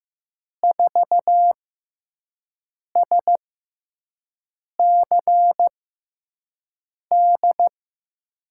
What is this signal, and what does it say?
Morse code, heard as a keyed tone, about 15 wpm, spaced as for 5 wpm, 700 Hz. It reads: 4SCD
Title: USA Call Signs - 6x Character Spacing - 15wpm